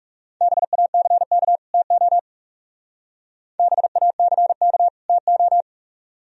Morse code, heard as a keyed tone, about 35 wpm, 700 Hz. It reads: BACK TO BACK TO
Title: Speed-Racing - Sets of 2 Words 30wpm